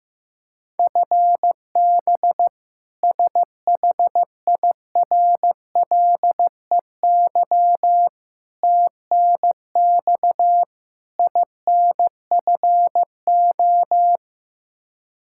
Morse code, heard as a keyed tone, about 15 wpm, 700 Hz. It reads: FB SHIRLEY TNX INFO
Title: K1USN SST QSO Practice 15wpm